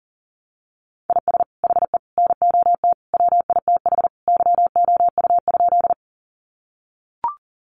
Morse code, heard as a keyed tone, about 37 wpm, 700 Hz. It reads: IS HE NOT WITH YOU?
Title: Sentences from Top 600 Words 40wpm